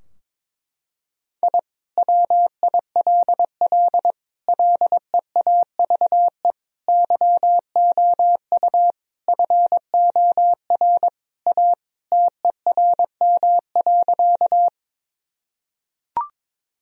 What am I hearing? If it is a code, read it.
Morse code, heard as a keyed tone, about 22 wpm, 700 Hz. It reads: I WILL LEAVE YOU FOR A TERM.
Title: Sentences from Top 1000 Words - Review Part 4 - 22wpm